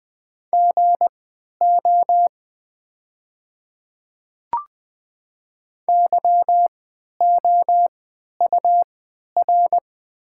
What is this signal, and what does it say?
Morse code, heard as a keyed tone, about 20 wpm, spaced as for 11 wpm, 700 Hz. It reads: GO YOUR
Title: 3x Character Spacing - Top 100 Words - 20wpm